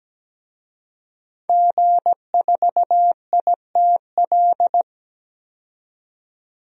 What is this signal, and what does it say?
Morse code, heard as a keyed tone, about 17 wpm, 700 Hz. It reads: G4ITL
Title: DXCC-100 Call Signs - 5 Characters - 17wpm